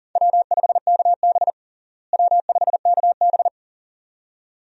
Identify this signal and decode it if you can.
Morse code, heard as a keyed tone, about 40 wpm, 700 Hz. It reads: W5KB W5KB
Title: USA Call Signs - 4 Characters - Repeated 2x - 40wpm